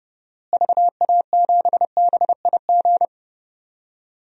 Morse code, heard as a keyed tone, about 30 wpm, 700 Hz. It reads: VA7BIG
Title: DXCC-100 Call Signs - 6 Characters - Repeated 3x - 30wpm